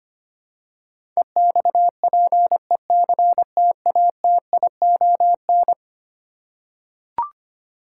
Morse code, heard as a keyed tone, about 25 wpm, 700 Hz. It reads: EXPECTATION
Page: morse-code-ninja-practice-25wpm